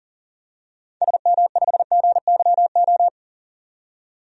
Morse code, heard as a keyed tone, about 40 wpm, 700 Hz. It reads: SM5GYO
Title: DXCC-100 Call Signs - 6 Characters - 40wpm